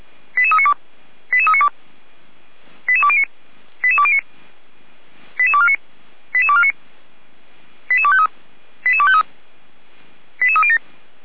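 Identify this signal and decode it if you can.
Five-tone selective call (ZVEI), heard as quick runs of five tones, tones of about 70 ms. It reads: 88392 88392 88209 88209 88249 88249 88253 88253 88307
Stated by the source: ZVEI five-tone call 88141 received on an SDR.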